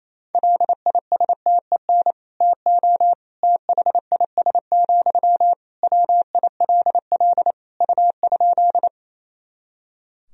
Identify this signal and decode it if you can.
Morse code, heard as a keyed tone, about 27 wpm, 700 Hz. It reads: LISTEN TO THIS, WILL U?
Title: Sentences from Top 400 Words and Abbreviations 28wpm